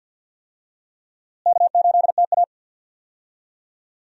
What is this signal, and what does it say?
Morse code, heard as a keyed tone, about 52 wpm, 700 Hz. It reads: K8TA
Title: USA Call Signs - 4 Characters - Repeated 2x - 50wpm